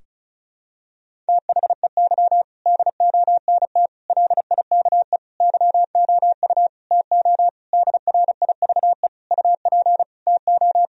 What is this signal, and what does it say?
Morse code, heard as a keyed tone, about 36 wpm, 700 Hz. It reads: THEY DONT LIKE YOU TO DRIVE UP TO
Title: Sentences from Top 1000 Words - Review Part 2 - 35wpm